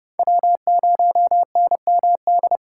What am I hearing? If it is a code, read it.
Morse code, heard as a keyed tone, about 39 wpm, 700 Hz. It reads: W0NMD